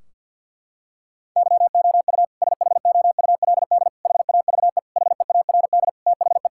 Morse code, heard as a keyed tone, about 50 wpm, 700 Hz. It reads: YOU SHOULD HAVE HEARD THE
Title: Sentences from Top 1600 Words 50wpm